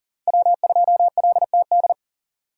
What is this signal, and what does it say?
Morse code, heard as a keyed tone, about 40 wpm, 700 Hz. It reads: W2LTD
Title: USA Call Signs - 2x Character Spacing - 40wpm